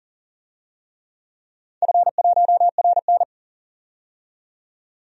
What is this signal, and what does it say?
Morse code, heard as a keyed tone, about 40 wpm, 700 Hz. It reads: F1RN
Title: DXCC-100 Call Signs - 4 Characters - 40wpm